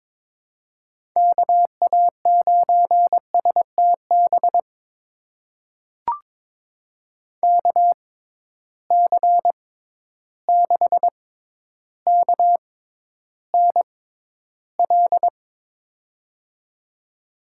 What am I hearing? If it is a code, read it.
Morse code, heard as a keyed tone, about 23 wpm, spaced as for 11 wpm, 700 Hz. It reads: KA9STB KC6KNL
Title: USA Call Signs - 6 Characters - 6x Character Spacing - 22wpm